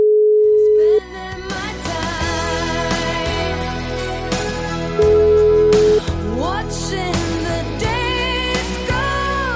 Ringback tone (Europe, 425 Hz), with noise.